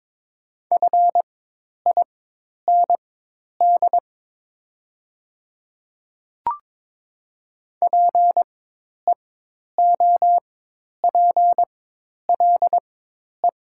Morse code, heard as a keyed tone, about 22 wpm, spaced as for 10 wpm, 700 Hz. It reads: FIND PEOPLE
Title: 4x Character Spacing - Top 100 Words - 22wpm